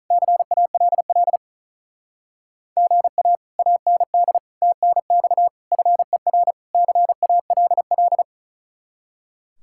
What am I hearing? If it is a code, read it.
Morse code, heard as a keyed tone, about 46 wpm, 700 Hz. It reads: CALL GA AND TNX FER CALL